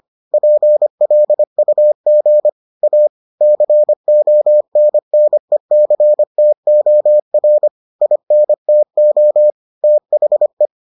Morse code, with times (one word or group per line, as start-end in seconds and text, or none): 0.3-2.5 PLUG
2.8-3.1 A
3.4-7.7 CONNECTOR
8.0-9.5 INTO
9.8-10.7 THE